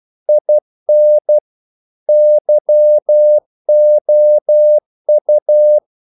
Morse code, heard as a keyed tone, about 12 wpm, 600 Hz. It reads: IN YOU
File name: CW / Practice / If12